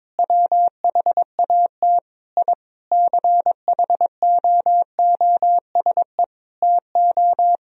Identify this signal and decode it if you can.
Morse code, heard as a keyed tone, about 22 wpm, 700 Hz. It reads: WHAT I CHOOSE TO
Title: Sentences from Top 1000 Words - Review Part 4 - 22wpm